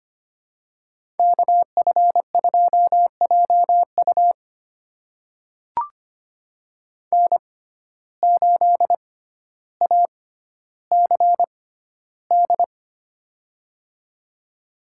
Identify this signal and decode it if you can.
Morse code, heard as a keyed tone, about 25 wpm, spaced as for 13 wpm, 700 Hz. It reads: KF2JU N8ACD
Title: USA Call Signs - 5 Characters - 6x Character Spacing - 25wpm